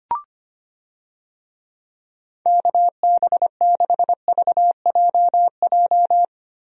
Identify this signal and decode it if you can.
Morse code, heard as a keyed tone, about 25 wpm, 700 Hz. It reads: KB6VJJ
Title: USA Call Signs - 6 Characters - Repeated 3x - 25wpm